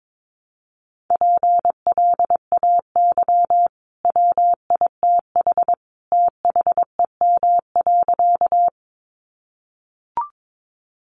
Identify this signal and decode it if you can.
Morse code, heard as a keyed tone, about 22 wpm, 700 Hz. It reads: PLAY WITH THEM.
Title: Sentences from Top 200 Words 22wpm